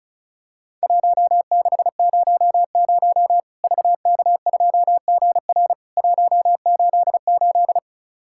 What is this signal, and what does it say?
Morse code, heard as a keyed tone, about 34 wpm, 700 Hz. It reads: 1600 VK2GR 188
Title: Worked All Europe DX Contest - QTC Exchange - 35wpm